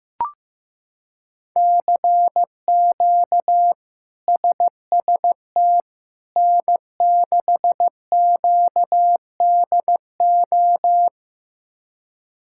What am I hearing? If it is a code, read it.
Morse code, heard as a keyed tone, about 15 wpm, 700 Hz. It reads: CQ SST N6QDO
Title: K1USN SST QSO Practice - 15wpm